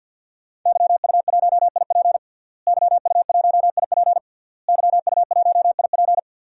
Morse code, heard as a keyed tone, about 46 wpm, 700 Hz. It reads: YU1IP YU1IP YU1IP
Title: DXCC-100 Call Signs - 5 Characters - Repeated 3x - 50wpm